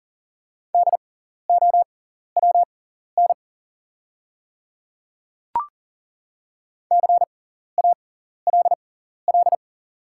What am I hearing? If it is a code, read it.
Morse code, heard as a keyed tone, about 40 wpm, spaced as for 14 wpm, 700 Hz. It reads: DOWN CALL
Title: 6x Character Spacing - Top 300 Words - 40wpm